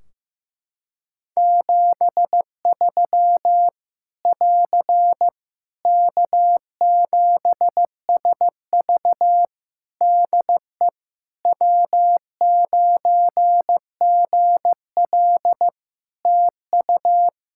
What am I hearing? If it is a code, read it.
Morse code, heard as a keyed tone, about 15 wpm, 700 Hz. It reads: 73 <AR> K7SV DE W9GL TU
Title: Practice QSO - Exchange 4 - Ending - 15wpm